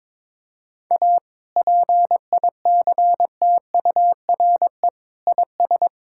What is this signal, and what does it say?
Morse code, heard as a keyed tone, about 22 wpm, 700 Hz. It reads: A PICTURE IS
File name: practice / 22wpm / Sets of 7 Words - 22wpm